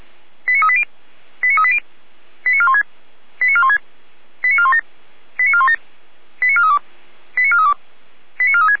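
Five-tone selective call (ZVEI), heard as quick runs of five tones, tones of about 70 ms. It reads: 89380 89380 89416 89416 89417 89417 89432 89432 89436
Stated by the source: ZVEI five-tone selective call received by SDR.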